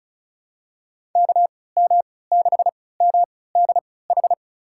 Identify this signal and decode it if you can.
Morse code, heard as a keyed tone, about 34 wpm, spaced as for 20 wpm, 700 Hz. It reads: KM6MDH
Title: USA Call Signs - 6 Characters - 3x Character Spacing - 35wpm